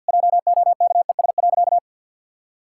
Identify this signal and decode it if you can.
Morse code, heard as a keyed tone, about 50 wpm, 700 Hz. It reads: JOKES.
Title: Pangram Challenge v2 - 50wpm